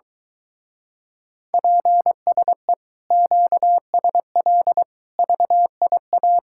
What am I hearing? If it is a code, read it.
Morse code, heard as a keyed tone, about 23 wpm, 700 Hz. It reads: PSE QSL VIA